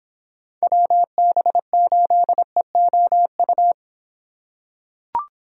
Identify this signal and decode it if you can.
Morse code, heard as a keyed tone, about 26 wpm, 700 Hz. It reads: WB8EOU